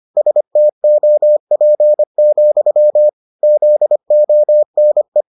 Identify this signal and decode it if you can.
Morse code, heard as a keyed tone, about 25 wpm, 600 Hz. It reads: STOP, ZONE